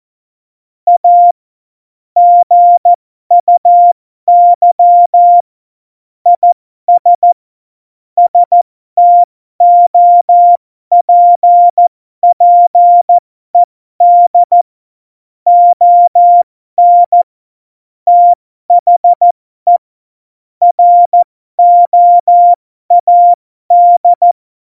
Morse code, heard as a keyed tone, about 14 wpm, spaced as for 12 wpm, 700 Hz. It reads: A GUY IS STOPPED ON THE ROAD
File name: Morsekurs / Witze